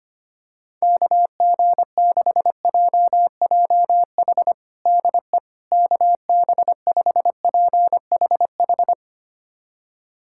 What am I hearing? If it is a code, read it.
Morse code, heard as a keyed tone, about 25 wpm, 700 Hz. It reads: KG6JJH DE KB5PHH
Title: USA Callsign DE Callsign 25wpm